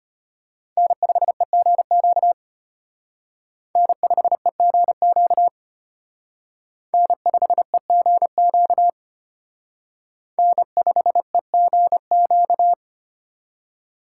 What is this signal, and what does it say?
Morse code, heard as a keyed tone, about 31 wpm, 700 Hz. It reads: N5EGQ N5EGQ N5EGQ N5EGQ
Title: Speed Racing - USA 1x3 Call Signs 25wpm